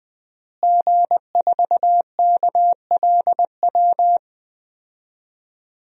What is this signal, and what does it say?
Morse code, heard as a keyed tone, about 20 wpm, 700 Hz. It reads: G4KLW
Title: DXCC-100 Call Signs - 5 Characters - Repeated 3x - 20wpm